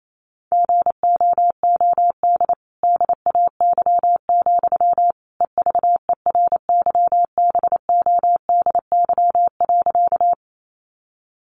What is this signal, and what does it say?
Morse code, heard as a keyed tone, about 28 wpm, 700 Hz. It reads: GOOD DAY, EVERYBODY.
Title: Sentences from Top 1400 Words 28wpm